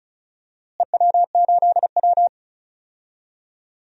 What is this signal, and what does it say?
Morse code, heard as a keyed tone, about 34 wpm, 700 Hz. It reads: EW8W
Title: DXCC-100 Call Signs - 4 Characters - Repeated 3x - 35wpm